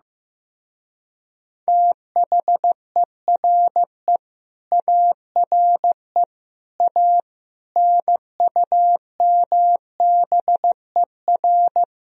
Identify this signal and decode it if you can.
Morse code, heard as a keyed tone, about 16 wpm, 700 Hz. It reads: THERE ARE A NUMBER
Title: Speed-Racing - Sets of 5 Words 15wpm